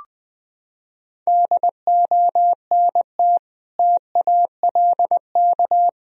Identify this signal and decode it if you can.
Morse code, heard as a keyed tone, about 20 wpm, 700 Hz. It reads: DONT TALK
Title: Sentences from Top 1000 Words - Review Part 4 - 20wpm